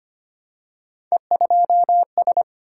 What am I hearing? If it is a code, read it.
Morse code, heard as a keyed tone, about 25 wpm, 700 Hz. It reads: E2S
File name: Three Letter-Number Combinations - 25wpm